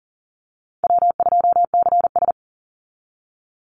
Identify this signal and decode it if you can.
Morse code, heard as a keyed tone, about 40 wpm, 700 Hz. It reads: W2CS
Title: Call Signs - 40wpm